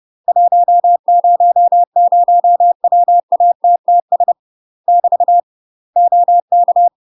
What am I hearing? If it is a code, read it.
Morse code, heard as a keyed tone, about 30 wpm, 700 Hz. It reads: 100 WATTS <BT> OK
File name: UK MASTER QSOs TRAINING / 30WPM / UK MASTER QSOs 0046